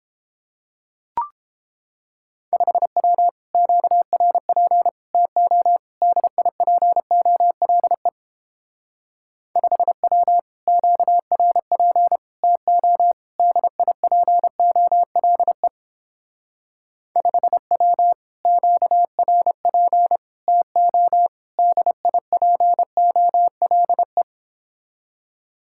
Morse code, heard as a keyed tone, about 29 wpm, 700 Hz. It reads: 5W QRP TO DIPOLE 5W QRP TO DIPOLE 5W QRP TO DIPOLE